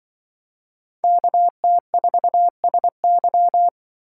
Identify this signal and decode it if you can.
Morse code, heard as a keyed tone, about 25 wpm, 700 Hz. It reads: KT4SY
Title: Speed Racing - USA 2x2 Call Signs 20wpm